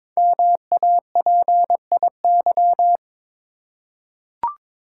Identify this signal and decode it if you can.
Morse code, heard as a keyed tone, about 22 wpm, 700 Hz. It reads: MAPIY
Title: Code Groups - 5 Letters - 22wpm